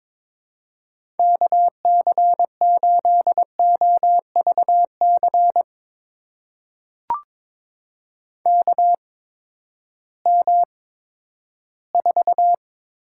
Morse code, heard as a keyed tone, about 22 wpm, spaced as for 13 wpm, 700 Hz. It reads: KC8OVC KM4
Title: USA Call Signs - 6 Characters - 8x Character Spacing - 22wpm